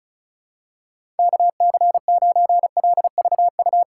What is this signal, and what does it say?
Morse code, heard as a keyed tone, about 36 wpm, 700 Hz. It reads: KC9LVU
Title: USA Call Signs - 6 Characters - 3x Character Spacing - 35wpm